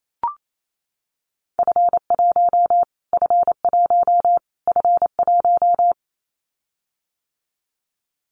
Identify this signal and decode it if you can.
Morse code, heard as a keyed tone, about 28 wpm, 700 Hz. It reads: F1 F1 F1
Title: Callsign Prefixes for Top 25 DXCC - Repeated 3x - 28wpm